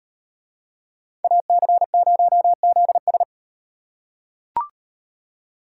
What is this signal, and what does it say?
Morse code, heard as a keyed tone, about 38 wpm, 700 Hz. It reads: AC0ZS